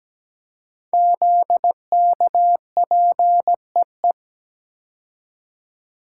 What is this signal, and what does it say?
Morse code, heard as a keyed tone, about 17 wpm, 700 Hz. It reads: ZKPEE
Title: Code Groups - 5 Letters - 17wpm